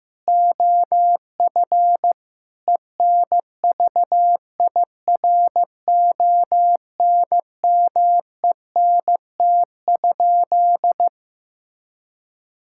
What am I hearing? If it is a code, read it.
Morse code, heard as a keyed tone, about 15 wpm, 700 Hz. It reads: OF ENVIRONMENT?
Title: Sentences from Top 1000 Words - Review Part 5 - 15wpm